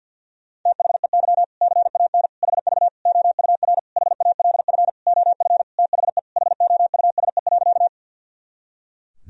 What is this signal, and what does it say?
Morse code, heard as a keyed tone, about 51 wpm, 700 Hz. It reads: THEY CAN HV OUR HALF OF THE HOUSE.
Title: Sentences from Top 400 Words and Abbreviations 50wpm